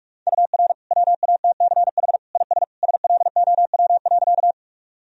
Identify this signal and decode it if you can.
Morse code, heard as a keyed tone, about 46 wpm, 700 Hz. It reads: UR WATCH IS SLOW.
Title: Sentences from Top 400 Words and Abbreviations 45wpm